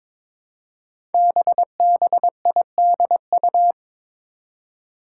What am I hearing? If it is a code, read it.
Morse code, heard as a keyed tone, about 22 wpm, 700 Hz. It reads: BBIDU
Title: Five Challenging Letter-Number Combinations 22wpm